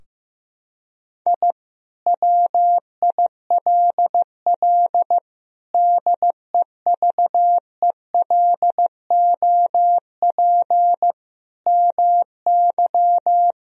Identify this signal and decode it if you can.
Morse code, heard as a keyed tone, about 15 wpm, 700 Hz. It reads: I WILL DEVELOP MY
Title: Sentences from Top 1000 Words - Review Part 2 - 15wpm